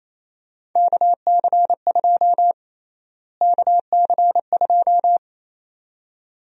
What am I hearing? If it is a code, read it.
Morse code, heard as a keyed tone, about 28 wpm, 700 Hz. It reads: KC2 KC2
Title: USA Call Sign Prefixes - Repeated 2x - 28wpm